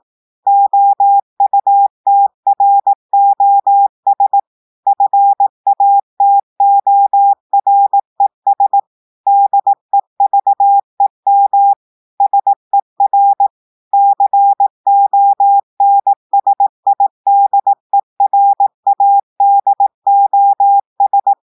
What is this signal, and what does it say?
Morse code, heard as a keyed tone, about 18 wpm, 800 Hz. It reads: OUTROS FATORES DEVEM SER CONSIDERADOS
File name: audios / portugais / portugais texte en clair 2 vitesse 18 mots minutes